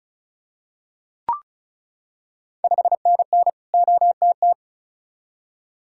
Morse code, heard as a keyed tone, about 36 wpm, 690 Hz. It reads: 5NN OTT
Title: Ontario QSO Party Exchange 35wpm